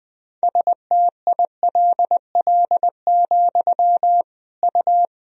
Morse code, heard as a keyed tone, about 21 wpm, 700 Hz. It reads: STILL, U
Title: Sentences from Top 300 Words and Abbreviations 20wpm